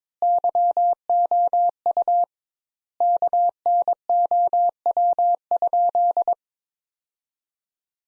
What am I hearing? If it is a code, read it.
Morse code, heard as a keyed tone, about 22 wpm, 700 Hz. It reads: YOU KNOW?
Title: Sentences Top 200 Words - 22wpm